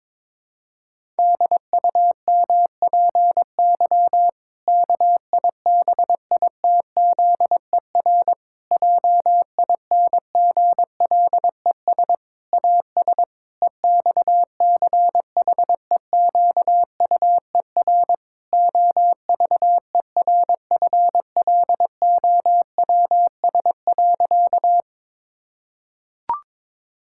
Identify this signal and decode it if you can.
Morse code, heard as a keyed tone, about 22 wpm, 700 Hz. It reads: DUMPY KIBITZER JINGLES AS EXCHEQUER OVERFLOWS.
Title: Pangram Challenge v2 - 22wpm